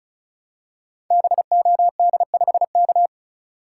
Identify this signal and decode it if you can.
Morse code, heard as a keyed tone, about 34 wpm, 700 Hz. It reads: BOD5K
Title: Code Groups - 5 Letter:Numbers - 35wpm